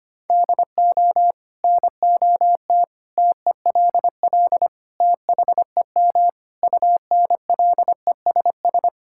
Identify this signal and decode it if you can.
Morse code, heard as a keyed tone, about 25 wpm, 700 Hz. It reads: DO NOT TELL THEM UNLESS
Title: Sentences from Top 1000 Words - Review Part 6 - 25wpm